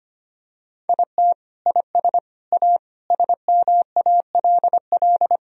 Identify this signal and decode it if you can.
Morse code, heard as a keyed tone, about 25 wpm, 700 Hz. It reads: IT IS A SMALL
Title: Sentences from Top 1000 Words - Review Part 4 - 25wpm